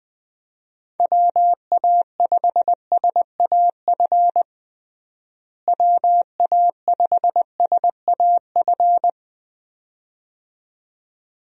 Morse code, heard as a keyed tone, about 21 wpm, 700 Hz. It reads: WA5SAF WA5SAF
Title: USA Call Signs - 6 Characters - Repeated 2x - 20wpm